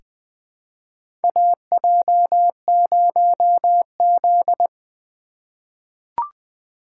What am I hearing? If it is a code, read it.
Morse code, heard as a keyed tone, about 20 wpm, 700 Hz. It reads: AJ0Z
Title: USA Call Signs - 4 Characters - 3x Character Spacing - 20wpm